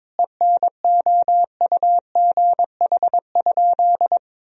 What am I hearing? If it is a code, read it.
Morse code, heard as a keyed tone, about 22 wpm, 700 Hz. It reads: ENOUGH?
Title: Sentences from Top 1000 Words - Review Part 1 - 22wpm